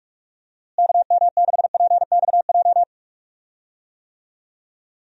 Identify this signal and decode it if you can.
Morse code, heard as a keyed tone, about 46 wpm, 700 Hz. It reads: KM6PXJ
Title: USA Call Signs - 6 Characters - Repeated 3x - 45wpm